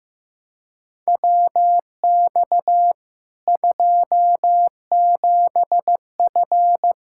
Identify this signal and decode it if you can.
Morse code, heard as a keyed tone, about 15 wpm, 700 Hz. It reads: WX 27F
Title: Practice QSO - Exchange 2 - Info - 15wpm